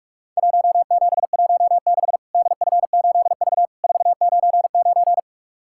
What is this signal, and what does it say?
Morse code, heard as a keyed tone, about 46 wpm, 700 Hz. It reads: 1716 DF8V 499
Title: Worked All Europe DX Contest - QTC Exchange - 45wpm